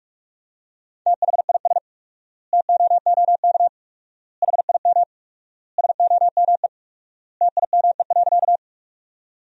Morse code, heard as a keyed tone, about 46 wpm, 700 Hz. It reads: THIS TOOK HIM SOME TIME.
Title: 4x Word-Spacing - Sentences from Top 300 Words - 45wpm